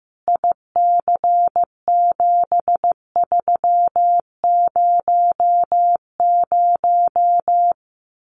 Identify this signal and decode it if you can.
Morse code, heard as a keyed tone, about 15 wpm, 700 Hz. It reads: IC7300
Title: Common QSO Phrases Encore 15wpm